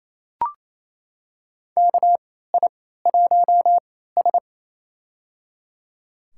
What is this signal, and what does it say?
Morse code, heard as a keyed tone, about 28 wpm, spaced as for 16 wpm, 700 Hz. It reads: KI1S
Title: USA Call Signs - 3x Character Spacing - 28wpm